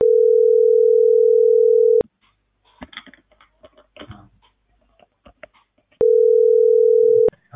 Ringback tone (North America), 440 + 480 Hz.